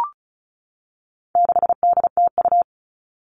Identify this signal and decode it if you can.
Morse code, heard as a keyed tone, about 34 wpm, 700 Hz. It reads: 6DTU
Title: All Cut Numbers - 4-Digit 35wpm